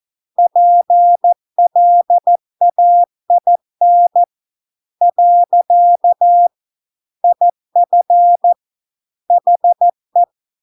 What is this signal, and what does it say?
Morse code, heard as a keyed tone, about 14 wpm, 700 Hz. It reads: PLAIN . IF HE